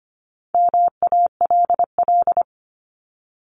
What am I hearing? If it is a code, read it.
Morse code, heard as a keyed tone, about 25 wpm, 700 Hz. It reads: MALL
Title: Top 1501-1750 Words 25wpm